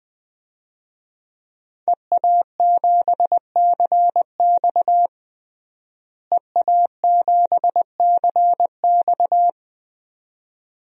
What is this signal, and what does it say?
Morse code, heard as a keyed tone, about 19 wpm, 700 Hz. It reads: EA7CX EA7CX
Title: DXCC-100 Call Signs - 5 Characters - Repeated 2x - 20wpm